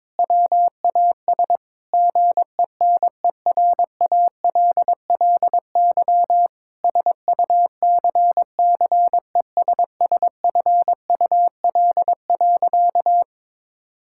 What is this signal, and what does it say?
Morse code, heard as a keyed tone, about 22 wpm, 700 Hz. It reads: WAS GENERALLY SUCCESSFUL.
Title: Sentences from Top 1000 Words - Review Part 6 - 22wpm